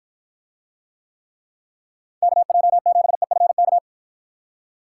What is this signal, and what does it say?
Morse code, heard as a keyed tone, about 53 wpm, 700 Hz. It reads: KJ7EFK